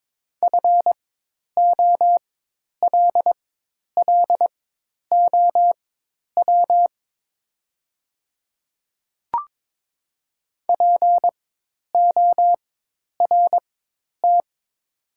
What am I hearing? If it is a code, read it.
Morse code, heard as a keyed tone, about 22 wpm, spaced as for 10 wpm, 700 Hz. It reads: FOLLOW PORT
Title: 4x Character Spacing - Top 200 Words - 22wpm